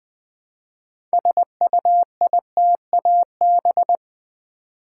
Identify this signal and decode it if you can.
Morse code, heard as a keyed tone, about 20 wpm, 700 Hz. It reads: SUITAB
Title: Build A Word - 2 Letters At A Time - 8-Letter Words - 20wpm